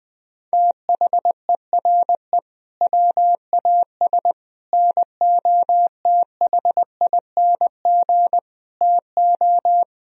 Morse code, heard as a keyed tone, about 20 wpm, 700 Hz. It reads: THERE WAS NOTHING TO
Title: Sentences from Top 1000 Words - Review Part 1 - 20wpm